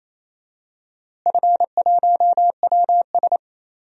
Morse code, heard as a keyed tone, about 28 wpm, 700 Hz. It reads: F1WS
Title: DXCC-100 Call Signs - 4 Characters - Repeated 2x - 28wpm